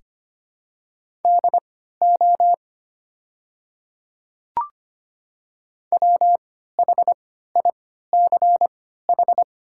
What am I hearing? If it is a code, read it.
Morse code, heard as a keyed tone, about 25 wpm, spaced as for 14 wpm, 700 Hz. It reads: DO WHICH
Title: 3x Character Spacing - Top 200 Words - 25wpm